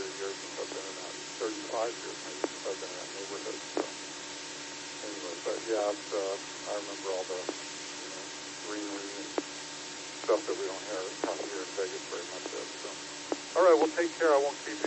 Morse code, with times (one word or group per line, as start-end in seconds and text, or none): none